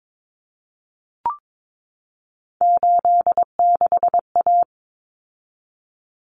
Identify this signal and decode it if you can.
Morse code, heard as a keyed tone, about 22 wpm, 700 Hz. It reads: 86A
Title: Common Cut Numbers - 3-Digit 22wpm